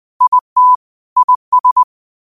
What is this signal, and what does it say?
Morse code, heard as a keyed tone, about 20 wpm, 1000 Hz. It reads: IT IS
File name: qotdfiles / 20wpm